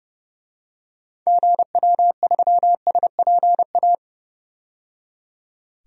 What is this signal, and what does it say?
Morse code, heard as a keyed tone, about 30 wpm, 700 Hz. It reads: GW3SPA
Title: DXCC-100 Call Signs - 6 Characters - 30wpm